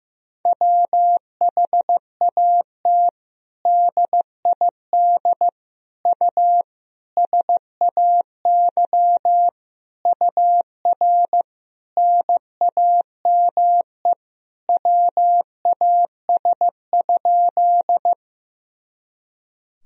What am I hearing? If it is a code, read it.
Morse code, heard as a keyed tone, about 16 wpm, 700 Hz. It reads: WHAT DID U SAY UR NAME WAS?
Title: Sentences from Top 200 Words and Abbreviations 15wpm